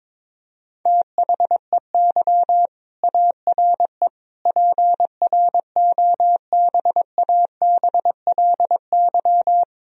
Morse code, heard as a keyed tone, about 22 wpm, 700 Hz. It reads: THEY ARE PROBABLY
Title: Sentences from Top 1000 Words - Review Part 4 - 22wpm